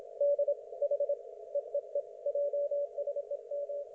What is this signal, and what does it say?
Morse code, heard as a keyed tone, about 27 wpm, 560 Hz. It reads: D H EEE JSEM
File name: Archiv / CUC179 / ok1fhi cul179